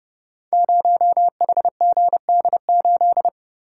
Morse code, heard as a keyed tone, about 28 wpm, 700 Hz. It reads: 0HGD8
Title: Code Groups - 5 Letter:Numbers - 30wpm